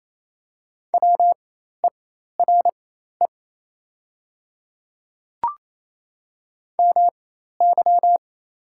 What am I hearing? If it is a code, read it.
Morse code, heard as a keyed tone, about 28 wpm, spaced as for 13 wpm, 700 Hz. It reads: WERE MY
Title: 4x Character Spacing - Top 100 Words - 28wpm